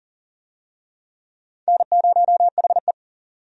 Morse code, heard as a keyed tone, about 40 wpm, 700 Hz. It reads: N0HE
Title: USA Call Signs - 4 Characters - 40wpm